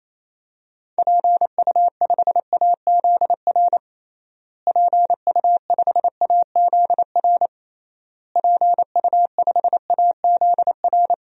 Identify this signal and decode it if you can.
Morse code, heard as a keyed tone, about 28 wpm, 700 Hz. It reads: PU5AZR PU5AZR PU5AZR
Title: DXCC-100 Call Signs - 6 Characters - Repeated 3x - 28wpm